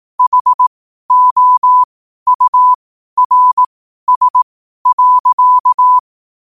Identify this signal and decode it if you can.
Morse code, heard as a keyed tone, about 18 wpm, spaced as for 12 wpm, 1000 Hz. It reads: HOURS.
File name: qotdfiles / 12wpm